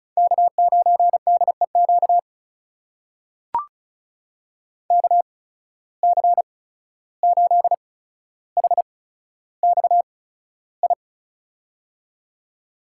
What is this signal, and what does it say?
Morse code, heard as a keyed tone, about 36 wpm, spaced as for 14 wpm, 700 Hz. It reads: K9DEQ KC8HXI
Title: USA Call Signs - 8x Character Spacing - 35wpm